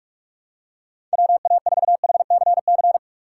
Morse code, heard as a keyed tone, about 46 wpm, 700 Hz. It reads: WA4HCC
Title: K1USN SST QSO Practice - 45wpm